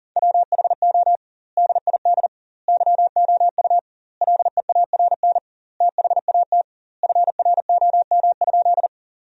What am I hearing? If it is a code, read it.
Morse code, heard as a keyed tone, about 40 wpm, 700 Hz. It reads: WHO DID YOU LEARN THAT FROM?
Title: 2x Word-Spacing - Sentences from Top 300 Words - 40wpm